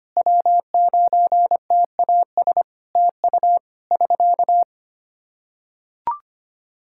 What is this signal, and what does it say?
Morse code, heard as a keyed tone, about 25 wpm, 700 Hz. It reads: W9TAS TU <SK>
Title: K1USN SST QSO Practice 25wpm